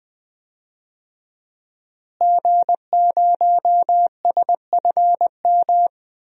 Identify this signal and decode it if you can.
Morse code, heard as a keyed tone, about 20 wpm, 700 Hz. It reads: G0SFM
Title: DXCC-100 Call Signs - 5 Characters - Repeated 2x - 20wpm